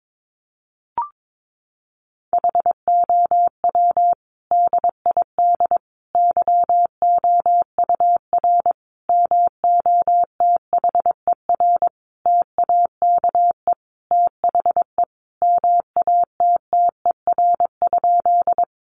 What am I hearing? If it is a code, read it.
Morse code, heard as a keyed tone, about 22 wpm, 700 Hz. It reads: HOW DID YOUR MOTHER TAKE THE MATTER?
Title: Sentences from Top 700 Words 22wpm